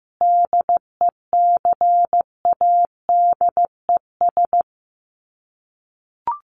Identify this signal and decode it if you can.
Morse code, heard as a keyed tone, about 15 wpm, 700 Hz. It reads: DECADES
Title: Sets of 5 Words Encore2 15wpm